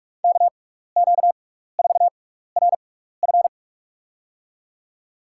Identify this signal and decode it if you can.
Morse code, heard as a keyed tone, about 44 wpm, spaced as for 15 wpm, 690 Hz. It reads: KQ4RF
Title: USA Call Signs - 5 Characters - 6x Character Spacing - 45wpm